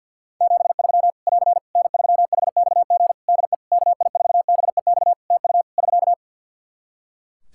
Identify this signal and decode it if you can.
Morse code, heard as a keyed tone, about 50 wpm, 700 Hz. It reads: 73 <AR> N3HXG DE KI4BEX TU <SK>
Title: K1USN SST QSO Practice 50wpm